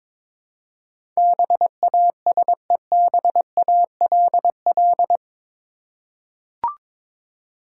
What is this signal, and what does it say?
Morse code, heard as a keyed tone, about 22 wpm, 700 Hz. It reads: BASEBALL
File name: practice / 22wpm / Eight Letter Words - 22wpm